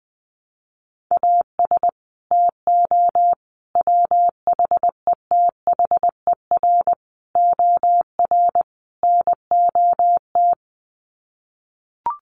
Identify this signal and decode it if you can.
Morse code, heard as a keyed tone, about 19 wpm, 700 Hz. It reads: AS TO WHETHER OR NOT
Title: Sets of 5 Words Encore 20wpm